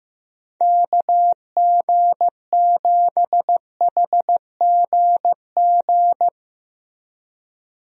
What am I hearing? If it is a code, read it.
Morse code, heard as a keyed tone, about 15 wpm, 700 Hz. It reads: KG7HGG
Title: USA Call Signs - 6 Characters - Repeated 3x - 15wpm